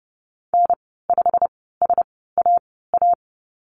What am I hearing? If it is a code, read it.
Morse code, heard as a keyed tone, about 30 wpm, 700 Hz. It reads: N 5 S A A
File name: Lesson 19g - Letter w - Call Signs 30wpm